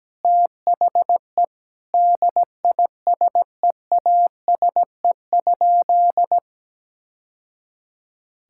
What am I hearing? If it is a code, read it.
Morse code, heard as a keyed tone, about 17 wpm, 700 Hz. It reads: THE DISEASE?
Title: Sentences from Top 1000 Words - Review Part 6 - 17wpm